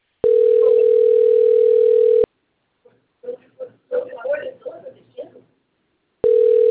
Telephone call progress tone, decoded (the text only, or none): ringback tone